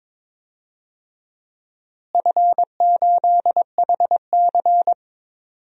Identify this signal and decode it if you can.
Morse code, heard as a keyed tone, about 22 wpm, 700 Hz. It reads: F8HC
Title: DXCC-100 Call Signs - 4 Characters - Repeated 3x - 22wpm